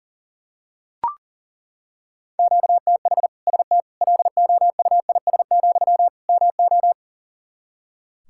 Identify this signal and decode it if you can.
Morse code, heard as a keyed tone, about 37 wpm, 700 Hz. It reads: QTH ST LOUIS, MO
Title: Practice QSO - Exchange 1 - Beginning - 40wpm